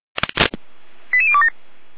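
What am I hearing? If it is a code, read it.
Five-tone selective call (ZVEI), heard as quick runs of five tones, tones of about 70 ms. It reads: 88027
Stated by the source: Five-tone ZVEI selective call, received on SDR.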